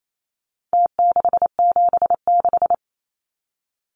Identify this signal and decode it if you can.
Morse code, heard as a keyed tone, about 28 wpm, 700 Hz. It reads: T676
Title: Numbers - Contest Style - 4-Digit 28wpm